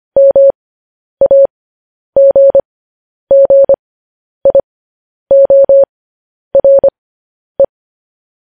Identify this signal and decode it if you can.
Morse code, heard as a keyed tone, about 25 wpm, spaced as for 10 wpm, 550 Hz. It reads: MAGGIORE
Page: news headlines in morse code at 10 wpm